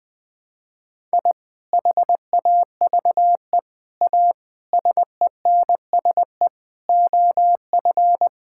Morse code, heard as a keyed tone, about 21 wpm, 700 Hz. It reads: I HAVE A SENSE OF
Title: Sentences with 3-Word Phrases - 20wpm